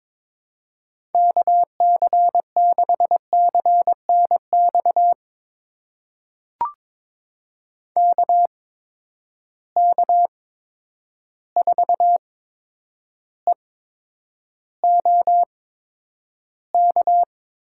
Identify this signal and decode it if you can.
Morse code, heard as a keyed tone, about 22 wpm, spaced as for 9 wpm, 700 Hz. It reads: KC6CNX KK4EOK
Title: USA Call Signs - 6 Characters - 8x Character Spacing - 22wpm